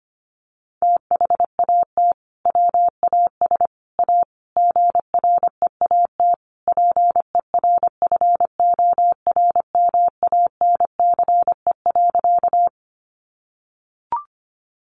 Morse code, heard as a keyed tone, about 25 wpm, 700 Hz. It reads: THAT WAS A GREAT PERFORMANCE.
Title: Sentences from Top 1000 Words 25wpm